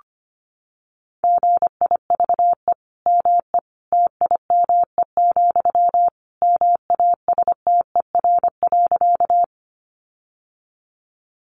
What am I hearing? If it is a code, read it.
Morse code, heard as a keyed tone, about 25 wpm, 700 Hz. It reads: GIVE ME TIME, MASTER.
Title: Sentences from Top 1300 Words 25wpm